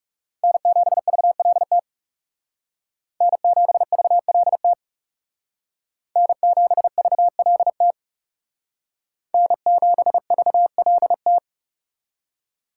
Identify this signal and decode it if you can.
Morse code, heard as a keyed tone, about 39 wpm, 700 Hz. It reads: N7VLT N7VLT N7VLT N7VLT